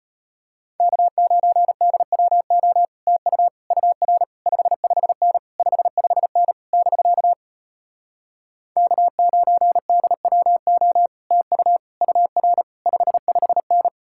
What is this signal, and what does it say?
Morse code, heard as a keyed tone, about 33 wpm, 700 Hz. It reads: K9DWO TU UR 55N 55N <BK> K9DWO TU UR 55N